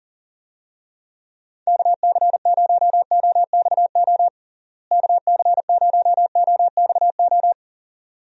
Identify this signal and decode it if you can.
Morse code, heard as a keyed tone, about 40 wpm, 700 Hz. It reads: KC0OXO KC0OXO
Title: USA Call Signs - 6 Characters - Repeated 2x - 40wpm